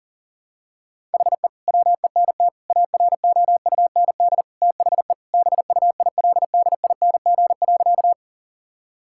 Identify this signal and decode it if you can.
Morse code, heard as a keyed tone, about 40 wpm, 700 Hz. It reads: HE WENT AROUND THE BUILDING.
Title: Sentences from Top 1000 Words - Review Part 5 - 40wpm